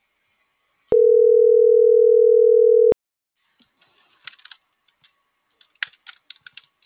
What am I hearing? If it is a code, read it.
Ringback tone (North America), 440 + 480 Hz.